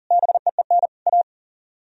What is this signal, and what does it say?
Morse code, heard as a keyed tone, about 48 wpm, 700 Hz. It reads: BEEN A
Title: Speed-Racing - Sets of 3 Words 40wpm